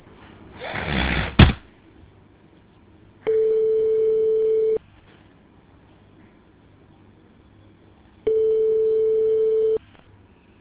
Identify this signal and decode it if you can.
Ringback tone (France), 440 Hz.